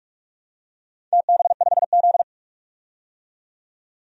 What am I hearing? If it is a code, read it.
Morse code, heard as a keyed tone, about 46 wpm, 700 Hz. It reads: TB5Z
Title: DXCC-100 Call Signs - 4 Characters - 45wpm